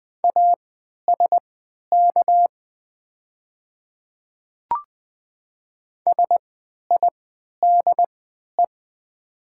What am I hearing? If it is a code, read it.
Morse code, heard as a keyed tone, about 20 wpm, spaced as for 11 wpm, 700 Hz. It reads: ASK SIDE
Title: 3x Character Spacing - Top 300 Words - 20wpm